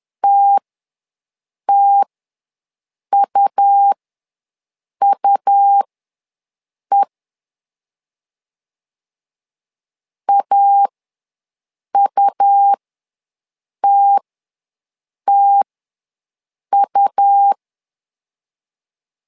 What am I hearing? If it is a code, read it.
Morse code, heard as a keyed tone, about 11 wpm, spaced as for 6 wpm, 800 Hz. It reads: TTUUE AUTTU
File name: LEZIONI / 2 UHDN / 2 U ws 10wpm 2017